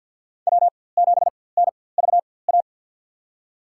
Morse code, heard as a keyed tone, about 49 wpm, spaced as for 23 wpm, 690 Hz. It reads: W7NVA
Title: USA Call Signs - 5 Characters - 4x Character Spacing - 50wpm